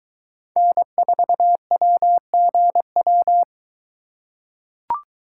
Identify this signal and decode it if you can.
Morse code, heard as a keyed tone, about 22 wpm, 700 Hz. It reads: N4WGW